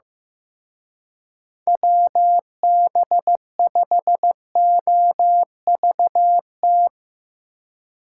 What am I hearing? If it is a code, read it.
Morse code, heard as a keyed tone, about 17 wpm, 700 Hz. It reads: WB5OVT
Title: Speed Racing - USA Call Signs 15wpm